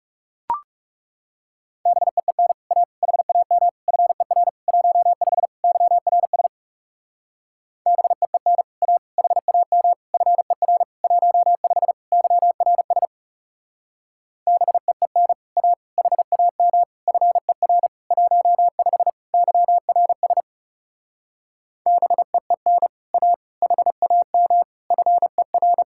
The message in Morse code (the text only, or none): BEEN A HAM FER 15 YRS BEEN A HAM FER 15 YRS BEEN A HAM FER 15 YRS BEEN A HAM FER